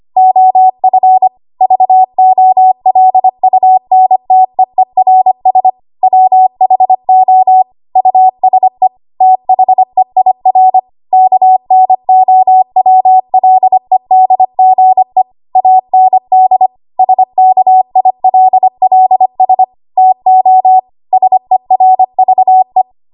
Morse code, heard as a keyed tone, about 25 wpm, 750 Hz. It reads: OF VOLUNTEERS WHO USE THEIR KNOWLEDGE AND SKILLS TO SERVE
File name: Morse / 25WPM